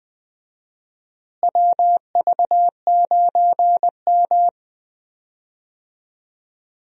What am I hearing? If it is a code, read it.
Morse code, heard as a keyed tone, about 21 wpm, 700 Hz. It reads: WV9M
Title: USA Call Signs - 4 Characters - Repeated 2x - 20wpm